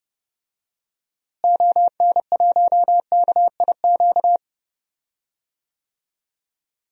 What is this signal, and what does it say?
Morse code, heard as a keyed tone, about 32 wpm, 700 Hz. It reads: ON1KIQ
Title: DXCC-100 Call Signs - 6 Characters - Repeated 2x - 30wpm